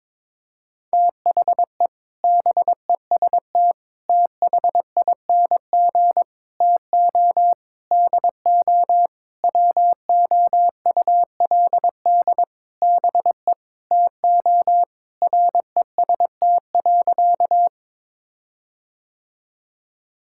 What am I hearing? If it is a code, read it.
Morse code, heard as a keyed tone, about 22 wpm, 700 Hz. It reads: THE BEST THING TO DO WOULD BE TO REST.
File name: practice / 22wpm / Sentences with 8-Word Phrases - 22wpm